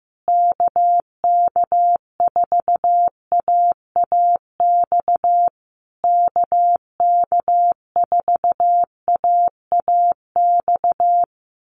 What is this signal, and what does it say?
Morse code, heard as a keyed tone, about 15 wpm, 700 Hz. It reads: KK4AAX KK4AAX
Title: USA Call Signs Repeated 3x 15wpm